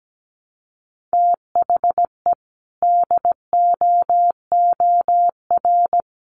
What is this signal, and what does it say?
Morse code, heard as a keyed tone, about 17 wpm, 700 Hz. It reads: THE DOOR
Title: Sets of 2 Words 17wpm